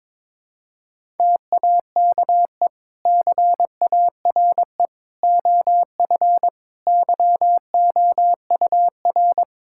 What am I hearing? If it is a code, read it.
Morse code, heard as a keyed tone, about 22 wpm, 700 Hz. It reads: TAKE CARE OF YOUR
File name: practice / 22wpm / Sentences with 2-Word Phrases - 22wpm